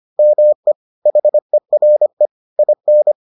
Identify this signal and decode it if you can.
Morse code, heard as a keyed tone, about 25 wpm, 600 Hz. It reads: ME HERE IN